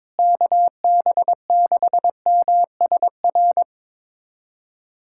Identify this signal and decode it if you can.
Morse code, heard as a keyed tone, about 22 wpm, 700 Hz. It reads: KB6MSR
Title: USA Callsign DE Callsign - 6 Characters - 22wpm